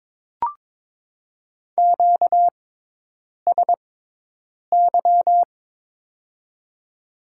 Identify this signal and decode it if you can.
Morse code, heard as a keyed tone, about 22 wpm, 700 Hz. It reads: Q S Y
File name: Top 90 QSO Elements - 6x Character Spacing - 22wpm